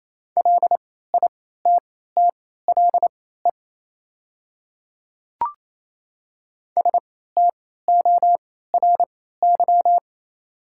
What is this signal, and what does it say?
Morse code, heard as a keyed tone, about 28 wpm, spaced as for 16 wpm, 700 Hz. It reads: LITTLE STORY
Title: 3x Character Spacing - Top 300 Words - 28wpm